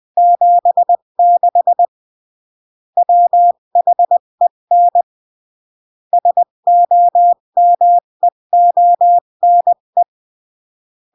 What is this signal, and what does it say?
Morse code, heard as a keyed tone, about 20 wpm, 700 Hz. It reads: 76 WHEN SOMEONE
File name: CW / jodorowsky / 82 commandments 20-15